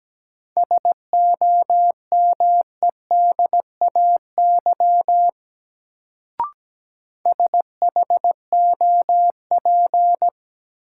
Morse code, heard as a keyed tone, about 17 wpm, 700 Hz. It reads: SOMEDAY SHOP
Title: Sentences from Top 1600 Words - New Word Drill Alt - 17wpm